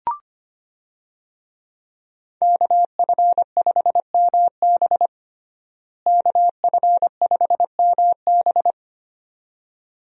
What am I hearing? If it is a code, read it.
Morse code, heard as a keyed tone, about 25 wpm, 700 Hz. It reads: KF5MB KF5MB
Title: USA Call Signs - 5 Characters - Repeated 2x - 25wpm